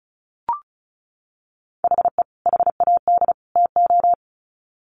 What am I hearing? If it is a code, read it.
Morse code, heard as a keyed tone, about 34 wpm, 700 Hz. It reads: HE HAD TO
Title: Sets of 3 Words 35wpm